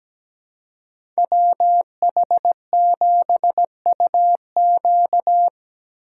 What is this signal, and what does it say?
Morse code, heard as a keyed tone, about 17 wpm, 700 Hz. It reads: WH7UQ
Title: USA Call Signs - 5 Characters - 4x Character Spacing - 17wpm